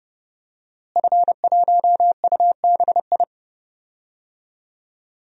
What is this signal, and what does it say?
Morse code, heard as a keyed tone, about 30 wpm, 700 Hz. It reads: F1UBI
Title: DXCC-100 Call Signs - 5 Characters - Repeated 3x - 30wpm